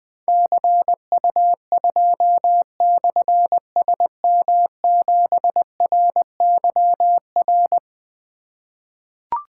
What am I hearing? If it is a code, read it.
Morse code, heard as a keyed tone, about 20 wpm, 700 Hz. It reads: CU2/SM7RYR
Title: Difficult Call Signs 20wpm